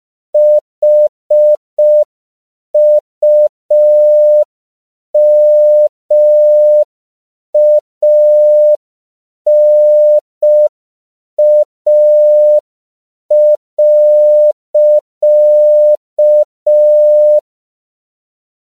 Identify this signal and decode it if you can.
Morse code, heard as a keyed tone, about 5 wpm, 600 Hz. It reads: HUMANA.
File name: CW / 5PPM